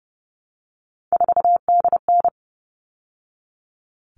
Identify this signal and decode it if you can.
Morse code, heard as a keyed tone, about 30 wpm, 700 Hz. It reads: 4DN
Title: All Cut Numbers - 3-Digit 30wpm